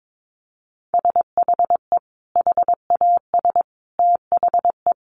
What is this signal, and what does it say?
Morse code, heard as a keyed tone, about 22 wpm, 700 Hz. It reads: SHE HAS THE
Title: Sentences from Top 1200 Words 22wpm